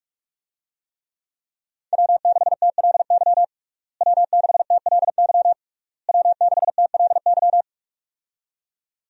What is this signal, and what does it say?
Morse code, heard as a keyed tone, about 46 wpm, 700 Hz. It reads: W6TLY W6TLY W6TLY
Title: USA Call Signs - 5 Characters - Repeated 3x - 45wpm